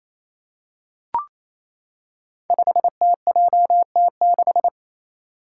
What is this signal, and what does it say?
Morse code, heard as a keyed tone, about 27 wpm, 700 Hz. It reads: 5TJT6
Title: Five Challenging Letter-Number Combinations 28wpm